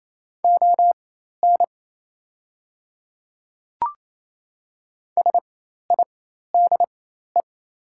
Morse code, heard as a keyed tone, about 27 wpm, spaced as for 13 wpm, 700 Hz. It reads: ON SIDE
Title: 4x Character Spacing - Top 100 Words - 28wpm